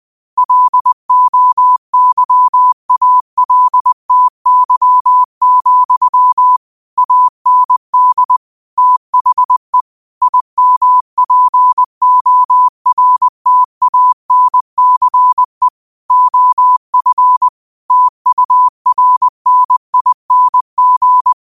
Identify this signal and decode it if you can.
Morse code, heard as a keyed tone, about 20 wpm, 1000 Hz. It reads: LOYALTY, AND THE IMPORTANCE OF TURNING
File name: qotdfiles / 20wpm